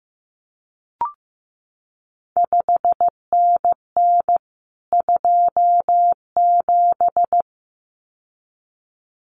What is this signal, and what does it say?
Morse code, heard as a keyed tone, about 15 wpm, 700 Hz. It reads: 5NN 27
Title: CQ WW DX Contest Exchange Encore - 15wpm